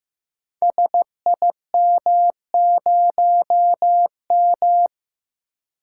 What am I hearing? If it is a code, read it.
Morse code, heard as a keyed tone, about 15 wpm, 700 Hz. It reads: SIM0M
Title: Code Groups - 5 Letter:Numbers - 15wpm